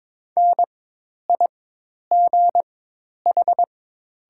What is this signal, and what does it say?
Morse code, heard as a keyed tone, about 22 wpm, spaced as for 10 wpm, 700 Hz. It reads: NIGH
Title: 4x Character Spacing - Top 300 Words - 22wpm